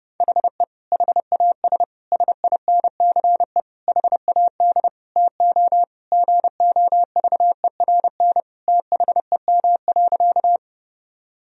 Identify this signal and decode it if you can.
Morse code, heard as a keyed tone, about 30 wpm, 700 Hz. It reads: HE HAS SINCE HAD TO GOVERN THEM.
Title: Sentences from Top 1000 Words - Review Part 2 - 30wpm